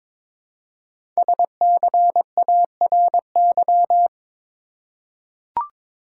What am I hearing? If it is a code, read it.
Morse code, heard as a keyed tone, about 22 wpm, 700 Hz. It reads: SCARY
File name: 5 Letter Words - First 3 Letters Given - 22wpm